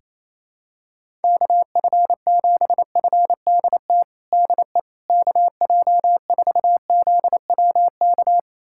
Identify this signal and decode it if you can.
Morse code, heard as a keyed tone, about 28 wpm, 700 Hz. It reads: KF7FDT DE KJ4ZWK
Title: USA Callsign DE Callsign - 6 Characters - 28wpm